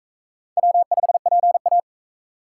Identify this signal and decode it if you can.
Morse code, heard as a keyed tone, about 39 wpm, 700 Hz. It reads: W5PA